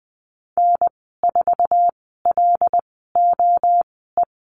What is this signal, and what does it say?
Morse code, heard as a keyed tone, about 20 wpm, spaced as for 14 wpm, 700 Hz. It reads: N4LOE
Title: 082 Lesson 14g - Number 2 - Call Signs 20wpm